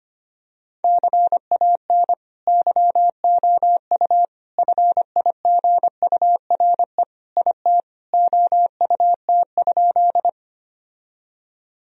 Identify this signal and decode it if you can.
Morse code, heard as a keyed tone, about 25 wpm, 700 Hz. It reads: CAN YOU FIGURE IT OUT?
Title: Sentences from Top 1000 Words - Review Part 2 - 25wpm